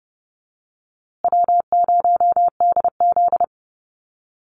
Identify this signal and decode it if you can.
Morse code, heard as a keyed tone, about 30 wpm, 700 Hz. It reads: W0DZ
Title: USA Call Signs Repeated 2x 30wpm